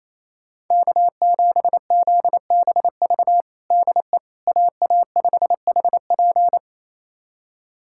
Morse code, heard as a keyed tone, about 28 wpm, 700 Hz. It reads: K7ZBV DE AA5HP
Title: USA Callsign DE Callsign - 5 Characters - 28wpm